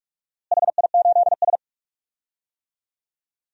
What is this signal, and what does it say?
Morse code, heard as a keyed tone, about 46 wpm, 700 Hz. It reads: HI8S
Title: DXCC-100 Call Signs - 4 Characters - Repeated 3x - 45wpm